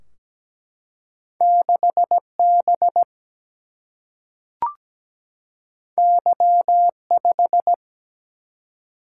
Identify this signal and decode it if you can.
Morse code, heard as a keyed tone, about 17 wpm, 700 Hz. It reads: 6B Y5
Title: Single-Letter Single-Number 17wpm